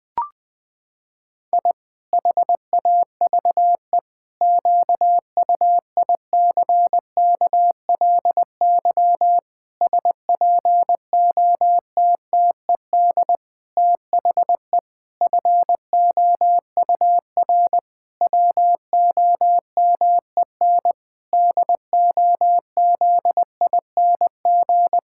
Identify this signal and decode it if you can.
Morse code, heard as a keyed tone, about 20 wpm, 700 Hz. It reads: I HAVE QUICKLY SPOTTED THE FOUR WOMEN DOZING
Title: Pangram Challenge v2 - 20wpm